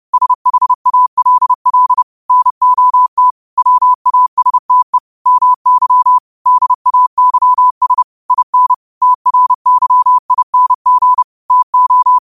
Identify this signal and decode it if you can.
Morse code, heard as a keyed tone, about 30 wpm, 1000 Hz. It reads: SHALL NOT WASTE MY DAYS IN TRYING TO
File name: qotdfiles / 30wpm